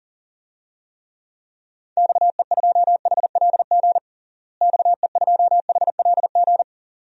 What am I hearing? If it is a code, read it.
Morse code, heard as a keyed tone, about 40 wpm, 700 Hz. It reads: XE2HLG XE2HLG
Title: DXCC-100 Call Signs - 6 Characters - Repeated 2x - 40wpm